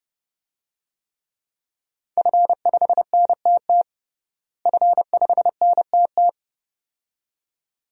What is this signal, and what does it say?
Morse code, heard as a keyed tone, about 30 wpm, 700 Hz. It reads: F5NTT F5NTT
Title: DXCC-100 Call Signs - 5 Characters - Repeated 2x - 30wpm